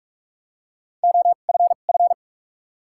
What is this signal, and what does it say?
Morse code, heard as a keyed tone, about 42 wpm, spaced as for 31 wpm, 700 Hz. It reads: OFF
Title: 2x Character Spacing - Top 200 Words - 45wpm